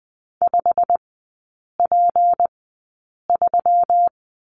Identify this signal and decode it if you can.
Morse code, heard as a keyed tone, about 19 wpm, spaced as for 8 wpm, 700 Hz. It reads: 5P3
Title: 184 Lesson 30e - Letter b - Three Characters 20wpm